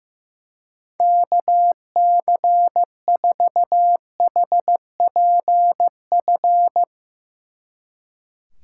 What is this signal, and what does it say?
Morse code, heard as a keyed tone, about 15 wpm, 700 Hz. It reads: KC4HPF
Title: Speed Racing - USA Call Signs 15wpm